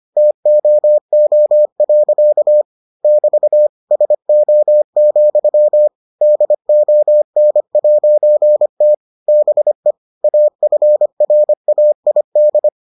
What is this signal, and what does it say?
Morse code, heard as a keyed tone, about 25 wpm, 600 Hz. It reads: TOO. <BT> SO, DON'T BE AFRAID